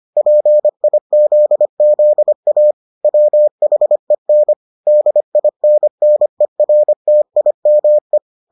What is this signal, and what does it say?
Morse code, heard as a keyed tone, about 25 wpm, 600 Hz. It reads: PIZZA WHEN DINNERTIME